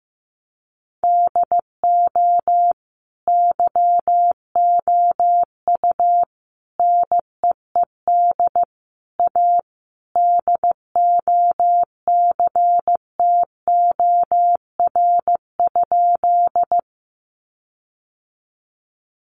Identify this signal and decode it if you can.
Morse code, heard as a keyed tone, about 15 wpm, 700 Hz. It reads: DO YOU NEED A DOCTOR?
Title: Sentences from Top 1500 Words 15wpm